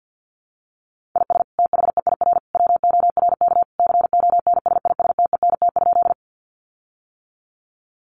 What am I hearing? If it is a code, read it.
Morse code, heard as a keyed tone, about 50 wpm, 700 Hz. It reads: IS THEIR WORK CONSISTENT?
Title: Sentences from Top 1500 Words 50wpm